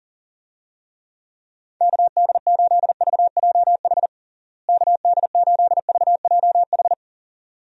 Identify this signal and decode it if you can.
Morse code, heard as a keyed tone, about 40 wpm, 700 Hz. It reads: KD8VJH KD8VJH
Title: USA Call Signs - 6 Characters - Repeated 2x - 40wpm